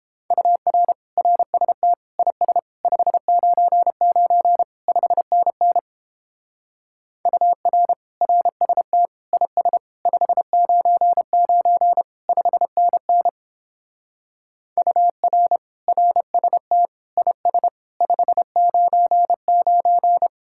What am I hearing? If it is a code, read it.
Morse code, heard as a keyed tone, about 31 wpm, 700 Hz. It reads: UR RST IS 599 5NN UR RST IS 599 5NN UR RST IS 599